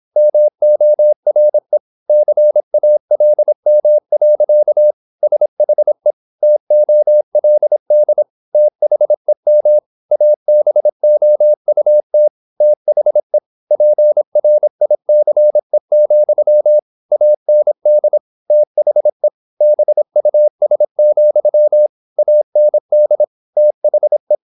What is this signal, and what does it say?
Morse code, heard as a keyed tone, about 26 wpm, 600 Hz. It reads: MORE CALM. SHE TOLD THEM ABOUT THE PRICE, AND THE BUS, AND THE
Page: Never 26wpm | CW med Gnister